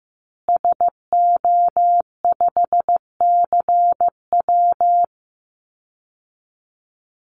Morse code, heard as a keyed tone, about 15 wpm, 700 Hz. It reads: SO5CW
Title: Call Signs Repeated 2x - 15wpm